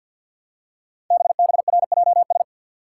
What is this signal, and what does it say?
Morse code, heard as a keyed tone, about 50 wpm, 700 Hz. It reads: BBRJS
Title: Code Groups - 5 Letters - 50wpm